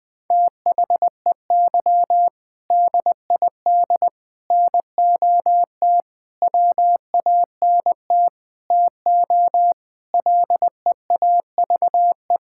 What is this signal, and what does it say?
Morse code, heard as a keyed tone, about 20 wpm, 700 Hz. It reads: THEY DID NOT WANT TO LEAVE
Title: Sentences with 5-Word Phrases - Spoken Beginning - 20wpm